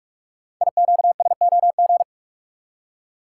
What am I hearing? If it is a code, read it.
Morse code, heard as a keyed tone, about 44 wpm, 700 Hz. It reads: IQSOG
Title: Code Groups - 5 Letters - 45wpm